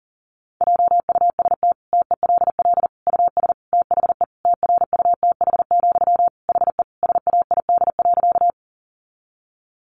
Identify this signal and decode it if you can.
Morse code, heard as a keyed tone, about 40 wpm, 700 Hz. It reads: JUST TELL US THE TRUTH, HE SAID.
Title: Sentences from Top 800 Words 40wpm